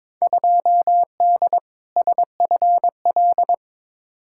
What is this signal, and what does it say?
Morse code, heard as a keyed tone, about 22 wpm, 700 Hz. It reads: 2D SFL
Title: Field Day Exchanges Encore - 22wpm